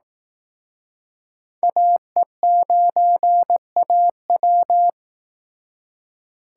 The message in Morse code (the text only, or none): AE9AW